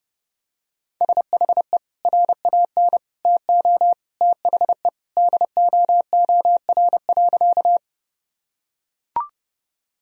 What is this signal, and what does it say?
Morse code, heard as a keyed tone, about 29 wpm, 700 Hz. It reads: SHE RAN TO THE DOOR.
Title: Sentences from Top 1000 Words - Review Part 3 - 30wpm